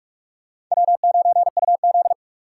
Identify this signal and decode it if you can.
Morse code, heard as a keyed tone, about 39 wpm, 700 Hz. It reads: W9UZ